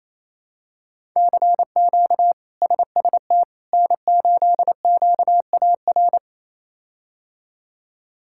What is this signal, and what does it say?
Morse code, heard as a keyed tone, about 28 wpm, 700 Hz. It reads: CQ SST N8QAR
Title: K1USN SST QSO Practice - 28wpm